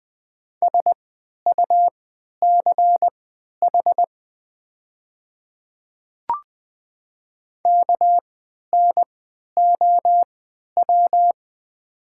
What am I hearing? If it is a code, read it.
Morse code, heard as a keyed tone, about 20 wpm, spaced as for 11 wpm, 700 Hz. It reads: SUCH KNOW
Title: 3x Character Spacing - Top 200 Words - 20wpm